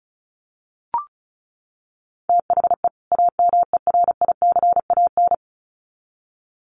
Morse code, heard as a keyed tone, about 34 wpm, 700 Hz. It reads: THE AMERICAN
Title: Sets of 2 Words 35wpm